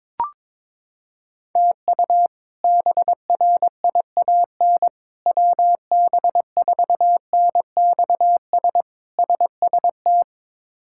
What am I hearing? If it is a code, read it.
Morse code, heard as a keyed tone, about 22 wpm, 700 Hz. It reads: TU BRIAN WB4NXS SST
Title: K1USN SST QSO Practice - 22wpm